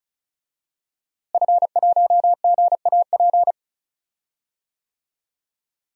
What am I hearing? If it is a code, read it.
Morse code, heard as a keyed tone, about 34 wpm, 700 Hz. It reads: F1GAP
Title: DXCC-100 Call Signs - 5 Characters - Repeated 3x - 35wpm